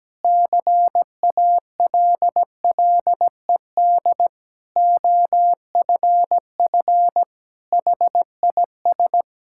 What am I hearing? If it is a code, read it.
Morse code, heard as a keyed tone, about 17 wpm, 700 Hz. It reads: CALLED OFF HIS
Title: Pangram Challenge v2 - 17wpm